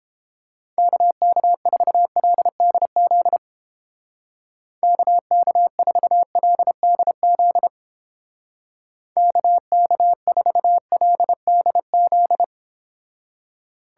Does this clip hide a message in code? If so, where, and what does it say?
Yes: Morse code from 0.8 to 12.4 s, heard as a keyed tone, about 31 wpm, 700 Hz. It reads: KK4LDZ KK4LDZ KK4LDZ